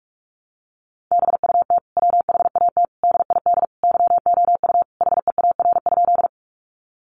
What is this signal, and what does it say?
Morse code, heard as a keyed tone, about 44 wpm, 700 Hz. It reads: BUT WHAT DID YOU HEAR?
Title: Sentences from Top 300 Words 45wpm